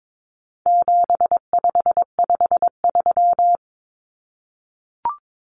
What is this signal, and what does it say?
Morse code, heard as a keyed tone, about 22 wpm, 700 Hz. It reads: 7553
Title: Numbers - 4-Digit 22wpm